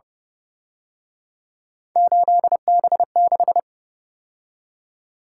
Morse code, heard as a keyed tone, about 29 wpm, 700 Hz. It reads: 8B6